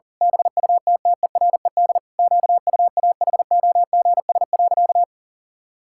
Morse code, heard as a keyed tone, about 40 wpm, 700 Hz. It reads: BUTTERED QUAHOGS.
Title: Pangram Challenge v2 - 40wpm